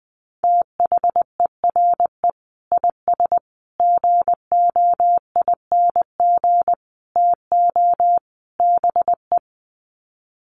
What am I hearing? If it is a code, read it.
Morse code, heard as a keyed tone, about 19 wpm, 700 Hz. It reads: THERE IS GOING TO BE
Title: Sets of 5 Words Encore 20wpm